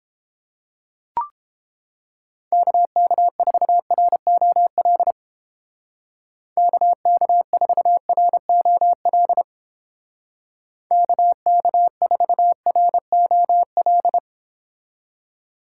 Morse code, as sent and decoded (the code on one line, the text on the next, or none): -.- -.- ....- .-. --- .-.. / -.- -.- ....- .-. --- .-.. / -.- -.- ....- .-. --- .-..
KK4ROL KK4ROL KK4ROL